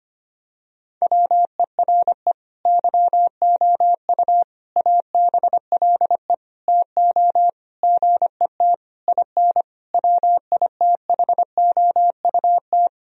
Morse code, heard as a keyed tone, about 25 wpm, 700 Hz. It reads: WERE YOU ABLE TO GET IN WITHOUT
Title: Sentences from Top 1000 Words - Review Part 4 - 25wpm